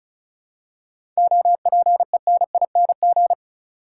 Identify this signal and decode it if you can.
Morse code, heard as a keyed tone, about 34 wpm, 700 Hz. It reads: OPENING
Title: Seven Letter Words - 35wpm